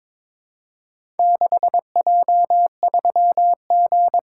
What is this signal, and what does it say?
Morse code, heard as a keyed tone, about 22 wpm, 700 Hz. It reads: 6J3G
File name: practice / 22wpm / Four Letter-Number Combinations - 22wpm